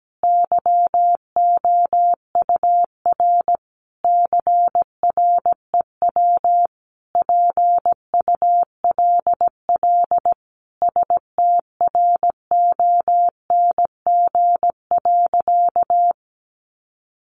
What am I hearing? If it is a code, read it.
Morse code, heard as a keyed tone, about 17 wpm, 700 Hz. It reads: YOUR CREW PULL STRONG.
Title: Sentences from Top 1500 Words 17wpm